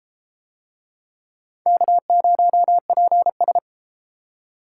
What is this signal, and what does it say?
Morse code, heard as a keyed tone, about 29 wpm, 700 Hz. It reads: K0PS